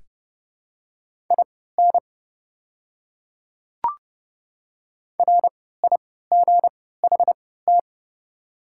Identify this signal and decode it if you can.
Morse code, heard as a keyed tone, about 30 wpm, spaced as for 17 wpm, 700 Hz. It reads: IN RIGHT
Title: 3x Character Spacing - Top 200 Words - 30wpm